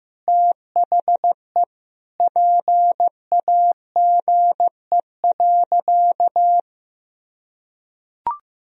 Morse code, heard as a keyed tone, about 16 wpm, 700 Hz. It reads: THE PAGE.
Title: Sentences from Top 1000 Words - Review Part 1 - 15wpm